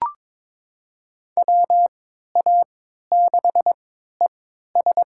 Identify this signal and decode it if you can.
Morse code, heard as a keyed tone, about 22 wpm, spaced as for 12 wpm, 700 Hz. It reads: WA6ES
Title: USA Call Signs - 5 Characters - 3x Character Spacing - 22wpm